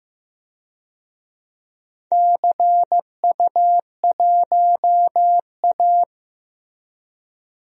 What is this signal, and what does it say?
Morse code, heard as a keyed tone, about 15 wpm, 700 Hz. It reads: CU1A
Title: DXCC-100 Call Signs - 4 Characters - Repeated 3x - 15wpm